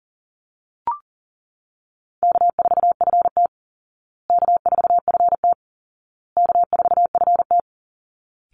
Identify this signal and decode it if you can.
Morse code, heard as a keyed tone, about 40 wpm, 700 Hz. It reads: K4FT K4FT K4FT
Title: Easy Call Signs Repeated 3x - 40wpm